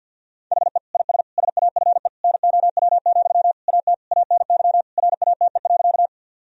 Morse code, heard as a keyed tone, about 50 wpm, 700 Hz. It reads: HE IS SAFE NOW, AT ANY RATE.
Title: Sentences from Top 1000 Words - Review Part 4 - 50wpm